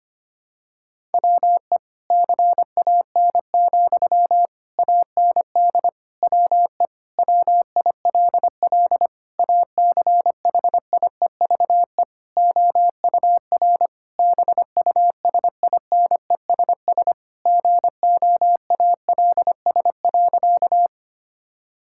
Morse code, heard as a keyed tone, about 25 wpm, 700 Hz. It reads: WE CAN, AND WE WILL ACHIEVE OUR BUSINESS GOALS.
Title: Sentences from Top 1000 Words - Review Part 6 - 25wpm